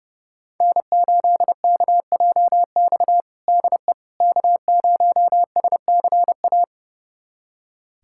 Morse code, heard as a keyed tone, about 32 wpm, 700 Hz. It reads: N8KJX DE K0SCA
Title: USA Callsign DE Callsign - 5 Characters - 30wpm